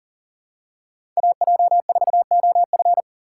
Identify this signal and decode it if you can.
Morse code, heard as a keyed tone, about 39 wpm, 700 Hz. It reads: AJ4OF
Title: USA Call Signs - 5 Characters - 3x Character Spacing - 40wpm